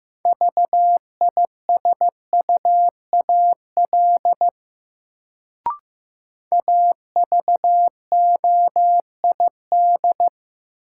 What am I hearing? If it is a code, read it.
Morse code, heard as a keyed tone, about 15 wpm, 700 Hz. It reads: VISUAL AVOID
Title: Sentences from Top 1000 Words - Review Part 6 - New Word Drill - 15wpm